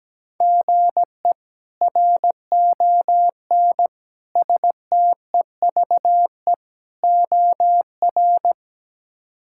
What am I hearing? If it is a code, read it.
Morse code, heard as a keyed tone, about 17 wpm, 700 Hz. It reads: GE RON STEVE OR
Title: K1USN SST QSO Practice - 17wpm